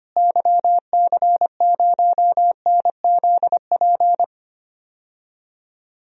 Morse code, heard as a keyed tone, about 25 wpm, 700 Hz. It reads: YC0NZP
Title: DXCC-100 Call Signs - 6 Characters - Repeated 2x - 25wpm